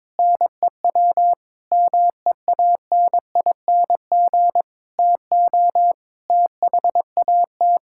Morse code, heard as a keyed tone, about 22 wpm, 700 Hz. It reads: NEW MEANING TO THAT
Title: Sentences from Top 1600 Words 22wpm